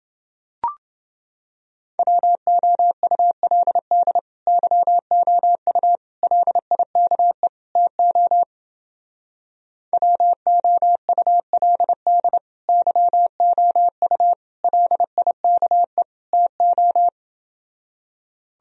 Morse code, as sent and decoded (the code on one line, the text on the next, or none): .-- --- ..- .-.. -.. / -.-- --- ..- / .-.. .. -.- . / - --- / .-- --- ..- .-.. -.. / -.-- --- ..- / .-.. .. -.- . / - ---
WOULD YOU LIKE TO WOULD YOU LIKE TO